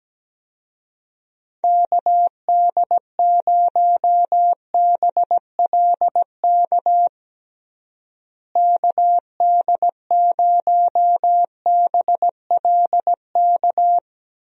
Morse code, heard as a keyed tone, about 17 wpm, 700 Hz. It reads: KD0BLK KD0BLK
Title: USA Call Signs - 6 Characters - Repeated 3x - 17wpm